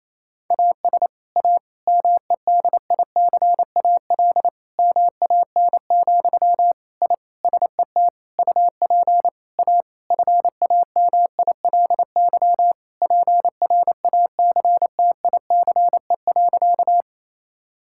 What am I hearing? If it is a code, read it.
Morse code, heard as a keyed tone, about 28 wpm, 700 Hz. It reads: AS A MEDICAL MAN, I SET UP A FAMILY PRACTICE.
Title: Sentences from Top 1000 Words - Review Part 5 - 28wpm